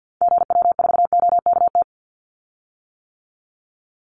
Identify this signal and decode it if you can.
Morse code, heard as a keyed tone, about 48 wpm, 700 Hz. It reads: GW4OKT
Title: Call Signs - 50wpm